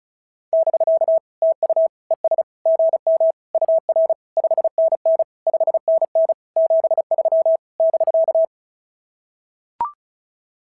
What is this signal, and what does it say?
Morse code, heard as a keyed tone, about 36 wpm, 650 Hz. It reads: <BK> TU ES GM UR 5NN 5NN 73 <BK>
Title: SOTA QSO Practice 35wpm